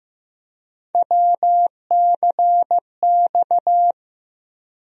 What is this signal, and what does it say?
Morse code, heard as a keyed tone, about 15 wpm, 700 Hz. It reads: WCX
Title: USA Call Sign Postfixes - 3 Letters 15wpm